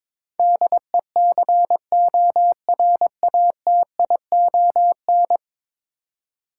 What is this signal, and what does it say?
Morse code, heard as a keyed tone, about 22 wpm, 700 Hz. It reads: DECORATION
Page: morse-code-ninja-practice-22wpm